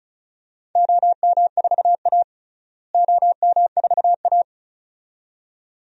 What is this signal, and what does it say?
Morse code, heard as a keyed tone, about 36 wpm, 700 Hz. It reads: OM4A OM4A
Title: DXCC-100 Call Signs - 4 Characters - Repeated 2x - 35wpm